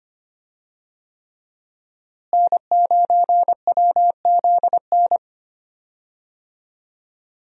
Morse code, heard as a keyed tone, about 25 wpm, 700 Hz. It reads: N9WZN
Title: USA Call Signs - 5 Characters - Repeated 3x - 25wpm